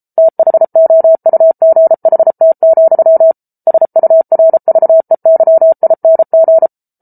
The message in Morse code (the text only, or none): THOUGHT, SURVEYING